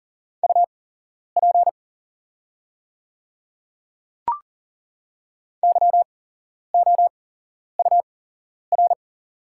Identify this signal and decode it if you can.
Morse code, heard as a keyed tone, about 40 wpm, spaced as for 11 wpm, 700 Hz. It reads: UP YOUR
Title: 8x Character Spacing - Top 300 Words - 40wpm